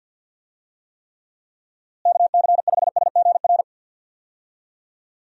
Morse code, heard as a keyed tone, about 48 wpm, 700 Hz. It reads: KC5SGR
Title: USA Call Signs - 6 Characters - Repeated 3x - 50wpm